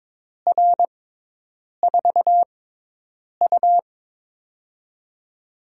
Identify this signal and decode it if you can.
Morse code, heard as a keyed tone, about 22 wpm, spaced as for 8 wpm, 700 Hz. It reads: R4U
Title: USA Call Signs - 6x Character Spacing - 22wpm